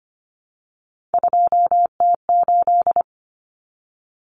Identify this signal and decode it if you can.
Morse code, heard as a keyed tone, about 25 wpm, 700 Hz. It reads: 2T8
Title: Common Cut Numbers - 3-Digit 25wpm